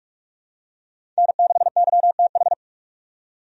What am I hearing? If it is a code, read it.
Morse code, heard as a keyed tone, about 46 wpm, 700 Hz. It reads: N6YTH
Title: USA Call Signs - 5 Characters - 45wpm